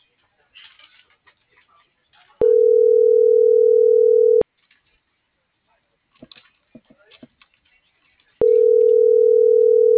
Ringback tone (North America), 440 + 480 Hz.